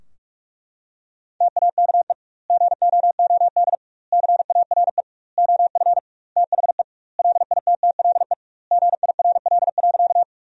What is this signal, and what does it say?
Morse code, heard as a keyed tone, about 45 wpm, 700 Hz. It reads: TAKE GOOD CARE OF THE LITTLE GIRL.
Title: 2x Word-Spacing - Sentences from Top 400 Words - 45wpm